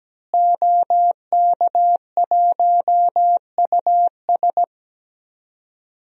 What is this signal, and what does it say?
Morse code, heard as a keyed tone, about 17 wpm, 700 Hz. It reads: OK1US
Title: DXCC-100 Call Signs - 5 Characters - Repeated 3x - 17wpm